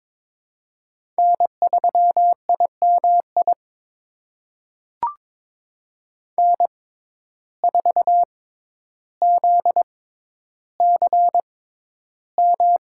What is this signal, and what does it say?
Morse code, heard as a keyed tone, about 22 wpm, spaced as for 11 wpm, 700 Hz. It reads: N3IMI N4ZCM
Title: USA Call Signs - 5 Characters - 6x Character Spacing - 22wpm